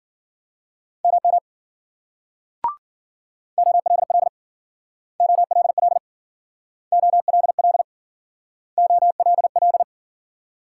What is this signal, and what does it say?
Morse code, heard as a keyed tone, about 53 wpm, 700 Hz. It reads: MM OLL OLL OLL OLL